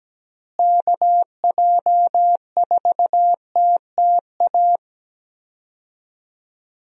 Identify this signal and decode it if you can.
Morse code, heard as a keyed tone, about 17 wpm, 700 Hz. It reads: KJ4TTA
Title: USA Call Signs - 6 Characters - Repeated 3x - 17wpm